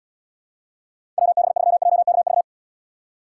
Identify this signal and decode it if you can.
Morse code, heard as a keyed tone, about 76 wpm, 700 Hz. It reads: WB3JKV